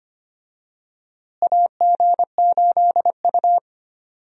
Morse code, heard as a keyed tone, about 25 wpm, 700 Hz. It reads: AG8U
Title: USA Call Signs - 4 Characters - Repeated 3x - 25wpm